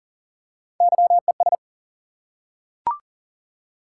Morse code, heard as a keyed tone, about 38 wpm, 700 Hz. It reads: YES
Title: Three Letter Words - 40wpm